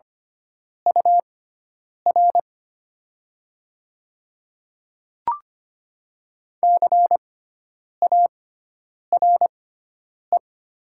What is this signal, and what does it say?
Morse code, heard as a keyed tone, about 25 wpm, spaced as for 9 wpm, 700 Hz. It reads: UR CARE
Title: 6x Character Spacing - Top 300 Words - 25wpm